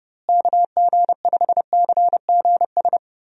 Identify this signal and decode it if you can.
Morse code, heard as a keyed tone, about 30 wpm, 700 Hz. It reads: KG5CGS
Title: USA Call Signs - 6 Characters - Repeated 3x - 30wpm